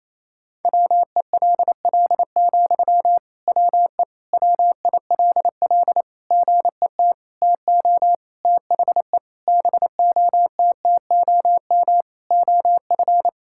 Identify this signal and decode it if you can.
Morse code, heard as a keyed tone, about 27 wpm, 700 Hz. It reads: WELL, WE WILL GET TO THE BOTTOM OF
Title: Sentences from Top 1000 Words - Review Part 6 - 28wpm